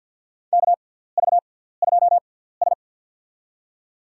Morse code, heard as a keyed tone, about 47 wpm, spaced as for 17 wpm, 700 Hz. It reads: KV2S
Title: USA Call Signs - 4 Characters - 6x Character Spacing - 50wpm